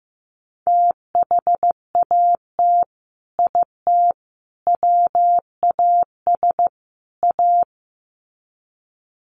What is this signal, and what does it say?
Morse code, heard as a keyed tone, about 15 wpm, 700 Hz. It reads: THAT IT WAS A
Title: Sets of 4 Words 15wpm